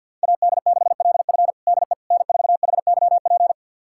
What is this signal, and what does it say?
Morse code, heard as a keyed tone, about 50 wpm, 700 Hz. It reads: AD6LF DE N4HYP
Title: USA Callsign DE Callsign - 5 Characters - 50wpm